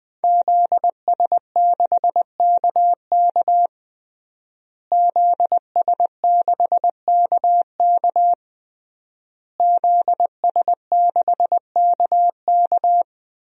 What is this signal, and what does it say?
Morse code, heard as a keyed tone, about 20 wpm, 700 Hz. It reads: ZS6KK ZS6KK ZS6KK
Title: DXCC-100 Call Signs - 5 Characters - Repeated 3x - 20wpm